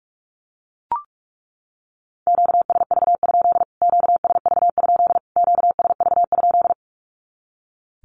Morse code, heard as a keyed tone, about 44 wpm, 700 Hz. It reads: QSV? QSV? QSV?
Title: Q-Codes Repeated 3x 45wpm